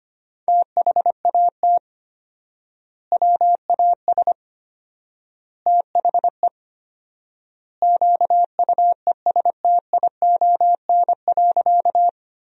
Morse code, heard as a keyed tone, about 25 wpm, 700 Hz. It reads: THAT WAS THE QUESTION.
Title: 4x Word-Spacing - Sentences from Top 400 Words - 25wpm